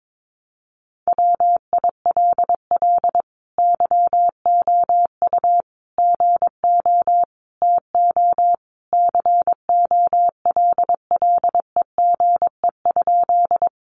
Morse code, heard as a keyed tone, about 22 wpm, 700 Hz. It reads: WILL YOU GO TO COLLEGE?
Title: Sentences from Top 1300 Words 22wpm